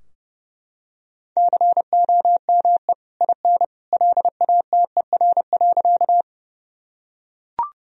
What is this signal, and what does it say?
Morse code, heard as a keyed tone, about 30 wpm, 700 Hz. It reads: COME IN LATER.
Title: Sentences from Top 1000 Words - Review Part 4 - 30wpm